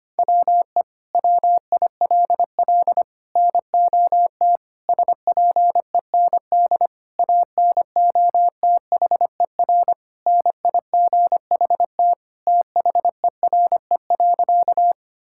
Morse code, heard as a keyed tone, about 25 wpm, 700 Hz. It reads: WE WILL NOT SPEND ANOTHER NIGHT THERE.
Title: Sentences from Top 1000 Words - Review Part 5 - 25wpm